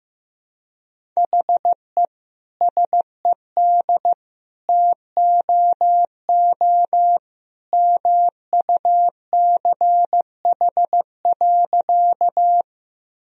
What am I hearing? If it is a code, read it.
Morse code, heard as a keyed tone, about 15 wpm, 700 Hz. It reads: HE SED TOO MUCH.
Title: Sentences from Top 200 Words and Abbreviations 15wpm